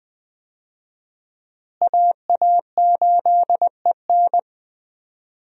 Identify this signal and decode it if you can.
Morse code, heard as a keyed tone, about 20 wpm, 700 Hz. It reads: AA8EN
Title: USA Call Signs - 5 Characters - Repeated 2x - 20wpm